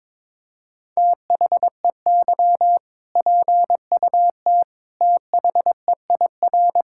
Morse code, heard as a keyed tone, about 22 wpm, 700 Hz. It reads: THEY PUT THEIR
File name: practice / 22wpm / Sentences with 8-Word Phrases - 22wpm